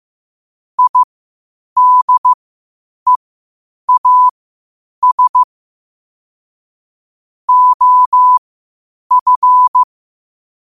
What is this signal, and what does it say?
Morse code, heard as a keyed tone, about 15 wpm, spaced as for 8 wpm, 1000 Hz. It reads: IDEAS OF
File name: qotdfiles / 8wpm